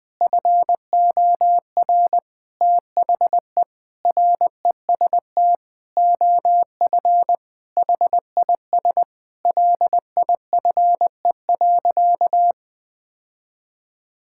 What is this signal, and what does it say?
Morse code, heard as a keyed tone, about 21 wpm, 700 Hz. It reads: FOR THE REST OF HIS LIFE.
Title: Sentences with 5-Word Phrases - 20wpm